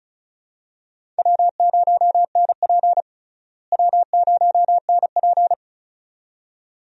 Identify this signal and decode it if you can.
Morse code, heard as a keyed tone, about 36 wpm, 700 Hz. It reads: W0NP W0NP
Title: USA Call Signs - 4 Characters - Repeated 2x - 35wpm